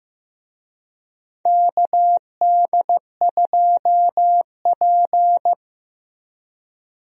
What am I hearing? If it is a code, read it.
Morse code, heard as a keyed tone, about 15 wpm, 700 Hz. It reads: KD2P
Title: USA Call Signs - 4 Characters - Repeated 3x - 15wpm